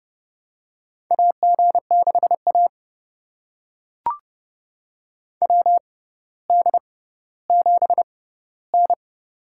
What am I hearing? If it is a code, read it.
Morse code, heard as a keyed tone, about 30 wpm, spaced as for 15 wpm, 700 Hz. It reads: AG6A WD7N
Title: USA Call Signs - 6x Character Spacing - 30wpm